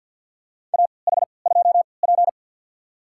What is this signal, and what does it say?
Morse code, heard as a keyed tone, about 48 wpm, spaced as for 28 wpm, 700 Hz. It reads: AH2P
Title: USA Call Signs - 4 Characters - 3x Character Spacing - 50wpm